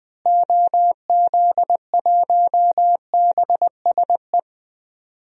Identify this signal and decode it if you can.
Morse code, heard as a keyed tone, about 20 wpm, 700 Hz. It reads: OZ1BSE
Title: DXCC-100 Call Signs - 6 Characters - Repeated 3x - 20wpm